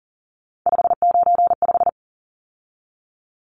Morse code, heard as a keyed tone, about 40 wpm, 700 Hz. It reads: 595
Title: Numbers - 3-Digit 40wpm